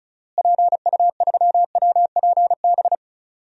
Morse code, heard as a keyed tone, about 36 wpm, 700 Hz. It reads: PU3WPB
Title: DXCC-100 Call Signs - 6 Characters - Repeated 3x - 35wpm